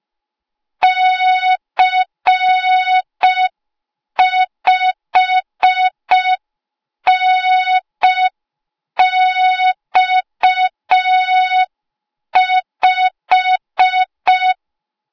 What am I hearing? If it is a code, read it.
Morse code, heard as a keyed tone, about 5 wpm, 3700 Hz. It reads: C5NX5